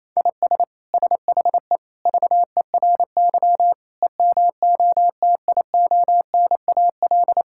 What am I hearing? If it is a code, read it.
Morse code, heard as a keyed tone, about 28 wpm, 700 Hz. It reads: IS SHE VERY EMOTIONAL
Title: Sentences from Top 1000 Words - Review Part 5 - 28wpm